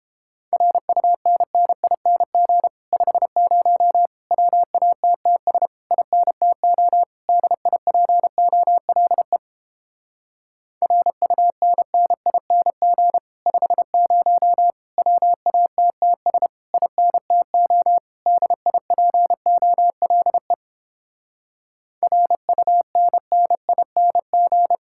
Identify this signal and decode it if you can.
Morse code, heard as a keyed tone, about 31 wpm, 700 Hz. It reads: RUNNING 50 WATTS INTO DIPOLE RUNNING 50 WATTS INTO DIPOLE RUNNING